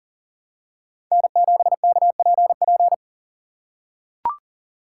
Morse code, heard as a keyed tone, about 40 wpm, 700 Hz. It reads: N7KPP
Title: USA Call Signs - 5 Characters - 4x Character Spacing - 40wpm